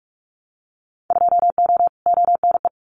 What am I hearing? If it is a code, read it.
Morse code, heard as a keyed tone, about 46 wpm, 700 Hz. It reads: 2O ONE
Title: Winter Field Day Exchanges Encore 45wpm